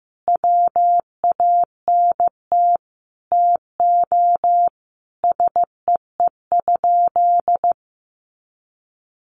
Morse code, heard as a keyed tone, about 15 wpm, 700 Hz. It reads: WANT TO SEE?
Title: Sentences from Top 1100 Words 15wpm